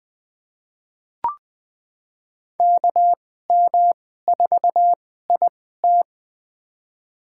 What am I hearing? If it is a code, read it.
Morse code, heard as a keyed tone, about 21 wpm, spaced as for 15 wpm, 700 Hz. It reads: KM4IT
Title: USA Call Signs - 5 Characters - 2x Character Spacing - 20wpm